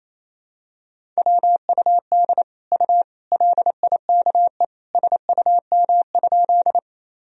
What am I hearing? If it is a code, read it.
Morse code, heard as a keyed tone, about 28 wpm, 700 Hz. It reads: WUD U LIKE SUM?
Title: Sentences from Top 100 Words and Abbreviations 28wpm